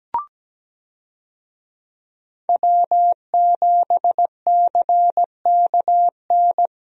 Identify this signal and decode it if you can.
Morse code, heard as a keyed tone, about 17 wpm, 700 Hz. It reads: W7CKN
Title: USA Call Signs - 5 Characters - 17wpm